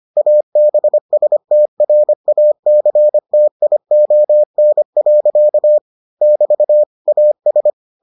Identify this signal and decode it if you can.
Morse code, heard as a keyed tone, about 25 wpm, 600 Hz. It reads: ABSTRACTION. <BT> AS